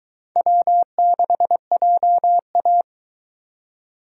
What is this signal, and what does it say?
Morse code, heard as a keyed tone, about 22 wpm, 700 Hz. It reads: W6JA